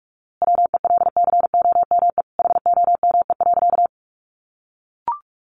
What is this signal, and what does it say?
Morse code, heard as a keyed tone, about 44 wpm, 700 Hz. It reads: WELCOME HOME.
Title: A Sound Of Thunder 45wpm